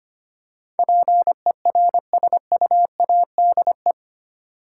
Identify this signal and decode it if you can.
Morse code, heard as a keyed tone, about 25 wpm, 700 Hz. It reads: PERSUADE
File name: Eight Letter Words - 25wpm